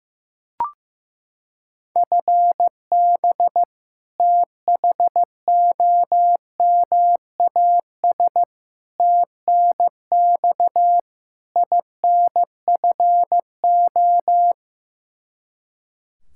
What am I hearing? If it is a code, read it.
Morse code, heard as a keyed tone, about 15 wpm, 700 Hz. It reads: FB THOMAS TNX INFO
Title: Practice QSO - Exchange 3 - More Info - 15wpm